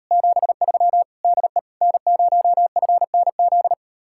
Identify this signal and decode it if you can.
Morse code, heard as a keyed tone, about 32 wpm, 700 Hz. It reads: 73 DE N0FNZ
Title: Speed-Racing Common QSO Phrases - 28wpm